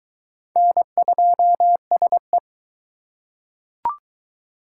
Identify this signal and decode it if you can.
Morse code, heard as a keyed tone, about 22 wpm, 700 Hz. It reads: N2SE